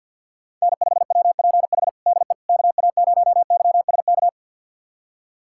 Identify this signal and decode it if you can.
Morse code, heard as a keyed tone, about 51 wpm, 700 Hz. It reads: N5WPH DE KA0YSK
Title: USA Callsign DE Callsign 50wpm